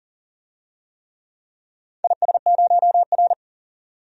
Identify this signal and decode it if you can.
Morse code, heard as a keyed tone, about 39 wpm, 700 Hz. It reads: IS0R
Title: DXCC-100 Call Signs - 4 Characters - 40wpm